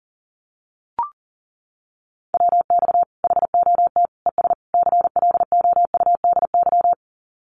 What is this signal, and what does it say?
Morse code, heard as a keyed tone, about 40 wpm, 700 Hz. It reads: WX HOT ES CLOUDY
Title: Common QSO Phrases 40wpm